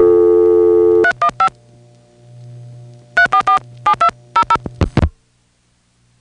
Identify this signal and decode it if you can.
DTMF, heard as telephone touch tones, tones of about 90 ms. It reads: 315 3447200